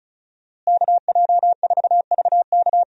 Morse code, heard as a keyed tone, about 34 wpm, 700 Hz. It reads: KJ4VK
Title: USA Call Signs - 5 Characters - Repeated 3x - 35wpm